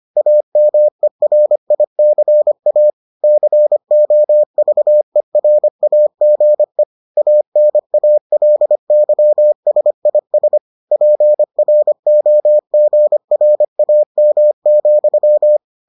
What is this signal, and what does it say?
Morse code, heard as a keyed tone, about 25 wpm, 600 Hz. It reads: AMERICA COVERAGE ANALYSIS PROGRAM,